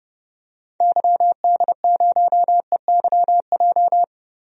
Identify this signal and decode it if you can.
Morse code, heard as a keyed tone, about 30 wpm, 700 Hz. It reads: YD0EYJ
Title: DXCC-100 Call Signs - 6 Characters - Repeated 2x - 30wpm